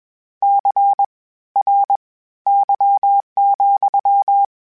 Morse code, heard as a keyed tone, about 21 wpm, spaced as for 14 wpm, 800 Hz. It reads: CRY,
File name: Misc / Misc-mp3s / WordPairs sp @20